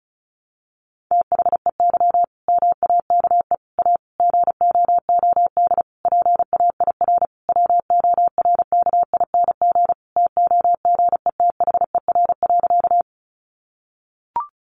Morse code, heard as a keyed tone, about 34 wpm, 700 Hz. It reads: THEY MAKE A GOOD PAIR WORKING TOGETHER.
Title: Sentences from Top 1200 Words 35wpm